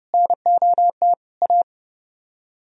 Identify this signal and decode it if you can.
Morse code, heard as a keyed tone, about 37 wpm, 700 Hz. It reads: NOT A
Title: Speed-Racing - Sets of 4 Words 30wpm